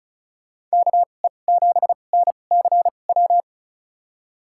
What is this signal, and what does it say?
Morse code, heard as a keyed tone, about 34 wpm, spaced as for 25 wpm, 700 Hz. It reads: KE7NCW
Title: USA Call Signs - 6 Characters - 2x Character Spacing - 35wpm